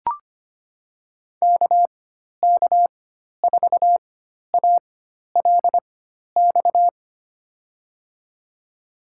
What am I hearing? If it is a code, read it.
Morse code, heard as a keyed tone, about 25 wpm, spaced as for 12 wpm, 700 Hz. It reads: KK4ALX
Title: USA Call Signs - 4x Character Spacing - 25wpm